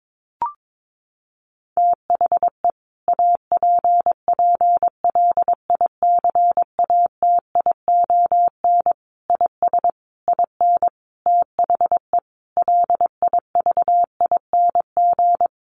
Morse code, heard as a keyed tone, about 22 wpm, 700 Hz. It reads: THE APPLICATION IS IN THE LIVING
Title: Sentences from Top 1200 Words 22wpm